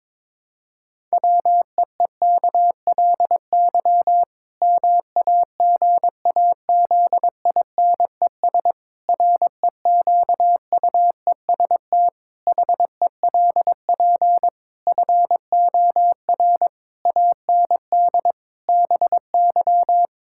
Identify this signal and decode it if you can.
Morse code, heard as a keyed tone, about 22 wpm, 700 Hz. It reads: WEEKLY MAGAZINES REQUEST HELP FOR AND BY
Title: Pangram Challenge v2 - 22wpm